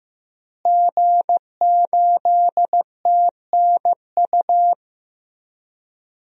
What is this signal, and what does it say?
Morse code, heard as a keyed tone, about 15 wpm, 700 Hz. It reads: G8TNU
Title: DXCC-100 Call Signs - 5 Characters - Repeated 3x - 15wpm